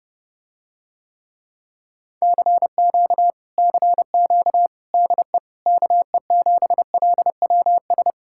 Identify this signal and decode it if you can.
Morse code, heard as a keyed tone, about 31 wpm, 700 Hz. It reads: CQ CQ DE KE7LWS